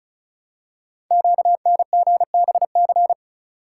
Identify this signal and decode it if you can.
Morse code, heard as a keyed tone, about 36 wpm, 700 Hz. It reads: QNGBC
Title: Code Groups - 5 Letters - 35wpm